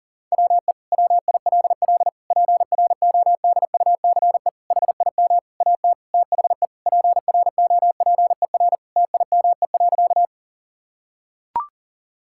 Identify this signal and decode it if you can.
Morse code, heard as a keyed tone, about 40 wpm, 700 Hz. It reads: WE WILL PRODUCE HIM AT THE PROPER TIME.
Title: Sentences from Top 1000 Words - Review Part 5 - 40wpm